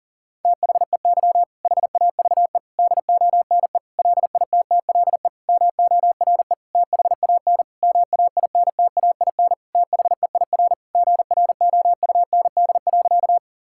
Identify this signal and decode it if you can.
Morse code, heard as a keyed tone, about 40 wpm, 700 Hz. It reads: THEY HAVE DONE LITTLE MORE THAN MAINTAIN THEIR GROUND.
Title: Sentences from Top 1000 Words - Review Part 5 - 40wpm